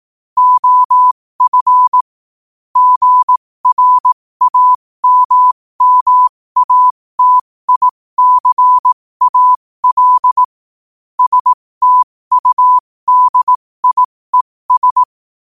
Morse code, heard as a keyed tone, about 18 wpm, spaced as for 15 wpm, 1000 Hz. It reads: OF GRAMMATICAL STUDIES
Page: New quotes every day in morse code at 15 Words per minute.